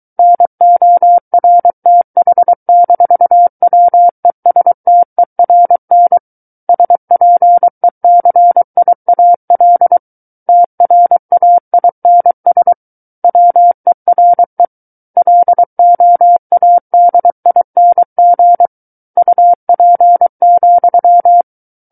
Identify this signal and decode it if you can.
Morse code, heard as a keyed tone, about 23 wpm, 700 Hz. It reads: NORTH-WESTERN SPECIAL TRAINS WERE LOADING UP,
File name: Book One / 14-Chapter 14 - 23 WPM